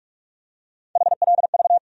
Morse code, heard as a keyed tone, about 46 wpm, 700 Hz. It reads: HLV
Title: USA Call Sign Postfixes - 3 Letters 45wpm